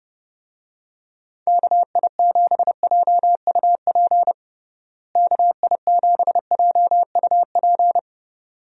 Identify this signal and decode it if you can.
Morse code, heard as a keyed tone, about 32 wpm, 700 Hz. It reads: KI7JUP KI7JUP
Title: USA Call Signs - 6 Characters - Repeated 2x - 30wpm